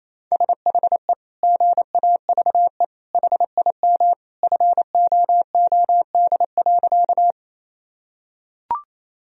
Morse code, heard as a keyed tone, about 28 wpm, 700 Hz. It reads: SHE GAVE HIM FOOD.
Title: Sentences from Top 1000 Words - Review Part 1 - 28wpm